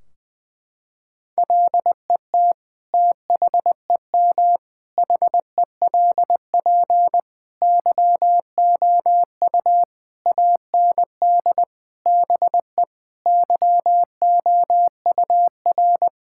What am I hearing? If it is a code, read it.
Morse code, heard as a keyed tone, about 20 wpm, 700 Hz. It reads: LET THEM HELP YOU AND BE YOUR
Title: Sentences from Top 1600 Words 20wpm